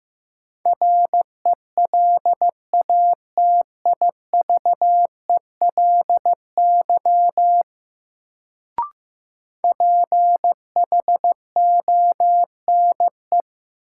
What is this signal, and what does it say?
Morse code, heard as a keyed tone, about 15 wpm, 700 Hz. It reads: RELATIVELY PHONE
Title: Sentences from Top 1100 Words - New Word Drill - 15wpm